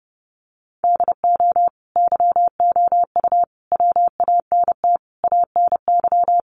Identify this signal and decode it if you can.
Morse code, heard as a keyed tone, about 30 wpm, 700 Hz. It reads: DO YOU WANT ANY
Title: Sentences from Top 400 Words 30wpm